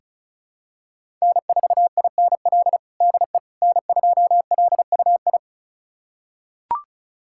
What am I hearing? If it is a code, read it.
Morse code, heard as a keyed tone, about 36 wpm, 700 Hz. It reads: N4INL DE N2LUI
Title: USA Callsign DE Callsign - 5 Characters - 35wpm